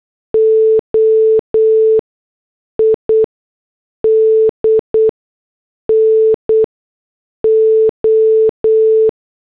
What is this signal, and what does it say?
Morse code, heard as a keyed tone, about 8 wpm, spaced as for 6 wpm, 440 Hz. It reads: OID NO
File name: morse / headlines